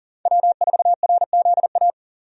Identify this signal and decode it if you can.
Morse code, heard as a keyed tone, about 38 wpm, 700 Hz. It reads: W4RZA
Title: USA Call Signs - 5 Characters - 40wpm